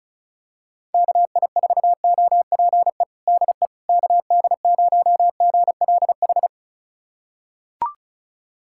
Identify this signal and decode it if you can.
Morse code, heard as a keyed tone, about 36 wpm, 700 Hz. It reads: KI4OPE DE KD0GLH
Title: USA Callsign DE Callsign 35wpm